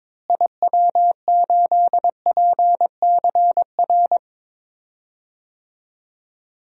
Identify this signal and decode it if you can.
Morse code, heard as a keyed tone, about 22 wpm, 700 Hz. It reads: IW8PCR
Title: DXCC-100 Call Signs - 6 Characters - Repeated 3x - 22wpm